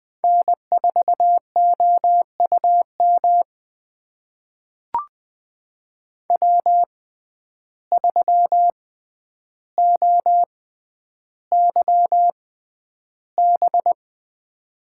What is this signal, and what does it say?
Morse code, heard as a keyed tone, about 21 wpm, spaced as for 10 wpm, 700 Hz. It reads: N4OUM W3OYB
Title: USA Call Signs - 5 Characters - 6x Character Spacing - 20wpm